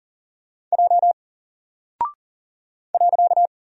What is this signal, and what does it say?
Morse code, heard as a keyed tone, about 40 wpm, 700 Hz. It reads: J .
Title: Single Character (Letter:Number:Punctuation:Prosign) 40wpm